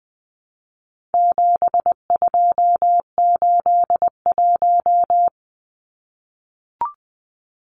Morse code, heard as a keyed tone, about 19 wpm, 700 Hz. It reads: 7281
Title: Numbers - 4-Digit 20wpm